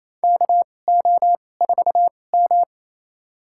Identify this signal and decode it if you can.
Morse code, heard as a keyed tone, about 28 wpm, spaced as for 20 wpm, 700 Hz. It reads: KO4M
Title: USA Call Signs - 4 Characters - 2x Character Spacing - 28wpm